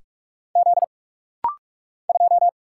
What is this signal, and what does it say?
Morse code, heard as a keyed tone, about 46 wpm, 700 Hz. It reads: Z 2
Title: Single Letter-Number Alt - Rapid-Fire 45wpm